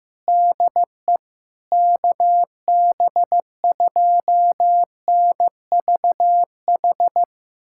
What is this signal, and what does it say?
Morse code, heard as a keyed tone, about 15 wpm, 700 Hz. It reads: DE KB2NVH
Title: USA Callsign DE Callsign - 6 Characters - 15wpm